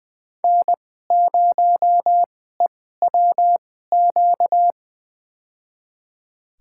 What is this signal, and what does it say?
Morse code, heard as a keyed tone, about 20 wpm, spaced as for 15 wpm, 700 Hz. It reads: N0EWQ
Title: USA Call Signs - 5 Characters - 2x Character Spacing - 20wpm